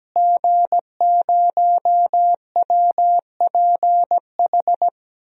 Morse code, heard as a keyed tone, about 17 wpm, 700 Hz. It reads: G0WPH
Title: DXCC-100 Call Signs - 5 Characters - Repeated 3x - 17wpm